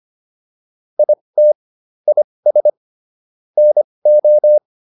Morse code, heard as a keyed tone, about 25 wpm, spaced as for 20 wpm, 600 Hz. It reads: IT IS NO